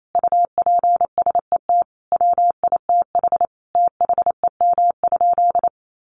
Morse code, heard as a keyed tone, about 28 wpm, 700 Hz. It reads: UPSET WITH THEM?
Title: Sentences from Top 800 Words 28wpm